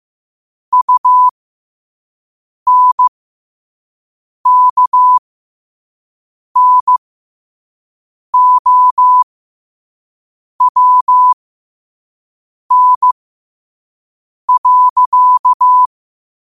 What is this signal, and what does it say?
Morse code, heard as a keyed tone, about 15 wpm, spaced as for 5 wpm, 1000 Hz. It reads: UNKNOWN.
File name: qotdfiles / 5wpm